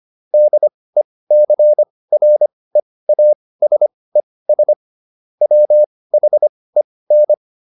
Morse code, heard as a keyed tone, about 25 wpm, spaced as for 18 wpm, 600 Hz. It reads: DECREASES WHEN